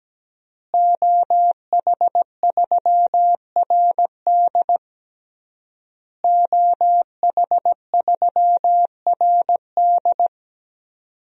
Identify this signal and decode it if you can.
Morse code, heard as a keyed tone, about 17 wpm, 700 Hz. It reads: OH3RD OH3RD
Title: DXCC-100 Call Signs - 5 Characters - Repeated 3x - 17wpm